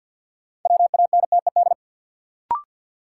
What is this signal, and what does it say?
Morse code, heard as a keyed tone, about 52 wpm, 700 Hz. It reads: WANTED
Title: Sets of 2 Words - New Word Drill - 50wpm